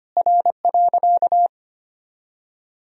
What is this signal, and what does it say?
Morse code, heard as a keyed tone, about 25 wpm, 700 Hz. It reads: R.
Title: Sentences from Top 200 Words and Abbreviations 25wpm